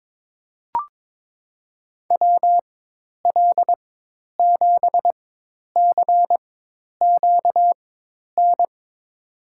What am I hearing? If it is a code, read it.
Morse code, heard as a keyed tone, about 22 wpm, spaced as for 10 wpm, 700 Hz. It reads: WL7CQN
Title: USA Call Signs - 4x Character Spacing - 22wpm